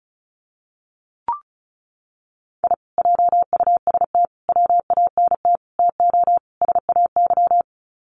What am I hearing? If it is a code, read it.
Morse code, heard as a keyed tone, about 34 wpm, 700 Hz. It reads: I JUST WANT TO SAY
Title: Sets of 5 Words Encore 35wpm